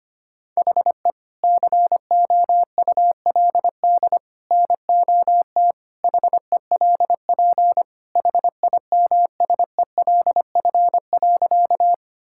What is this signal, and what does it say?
Morse code, heard as a keyed tone, about 25 wpm, 700 Hz. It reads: HE COULD NOT HELP HIMSELF.
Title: Sentences from Top 1600 Words 25wpm